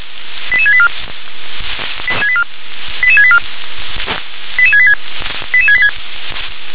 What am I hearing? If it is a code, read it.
Five-tone selective call (ZVEI), heard as quick runs of five tones, tones of about 70 ms. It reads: 88684 88684 88686 88686